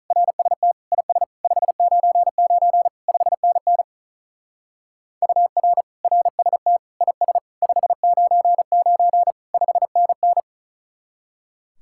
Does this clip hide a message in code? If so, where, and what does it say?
Yes: Morse code from 0.1 to 10.4 s, heard as a keyed tone, about 45 wpm, 700 Hz. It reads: RST IS 599 5NN UR RST IS 599 5NN